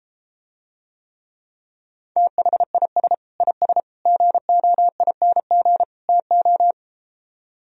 Morse code, heard as a keyed tone, about 33 wpm, 700 Hz. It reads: THIS IS GOING TO